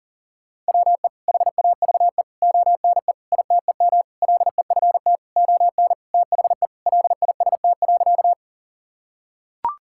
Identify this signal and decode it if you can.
Morse code, heard as a keyed tone, about 40 wpm, 700 Hz. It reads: WE HAVE ONE ITEM LEFT ON THE LIST.
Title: Sentences from Top 1000 Words - Review Part 6 - 40wpm